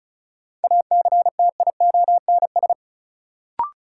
Morse code, heard as a keyed tone, about 36 wpm, 700 Hz. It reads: ACTIONS
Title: Sentences from Top 1000 Words - Review Part 6 - New Word Drill - 35wpm